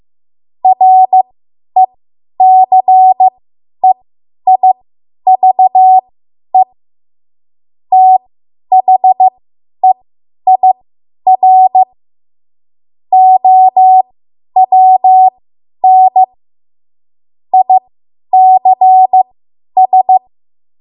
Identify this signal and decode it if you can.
Morse code, heard as a keyed tone, about 15 wpm, spaced as for 10 wpm, 750 Hz. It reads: RECEIVE THEIR OWN ICS